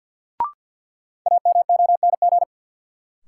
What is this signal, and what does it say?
Morse code, heard as a keyed tone, about 48 wpm, 700 Hz. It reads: AMONG
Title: Sets of 2 Words - New Word Drill - 50wpm